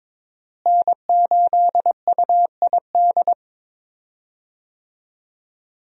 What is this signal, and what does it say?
Morse code, heard as a keyed tone, about 22 wpm, 700 Hz. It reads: N8UID
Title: USA Call Signs - 5 Characters - Repeated 2x - 22wpm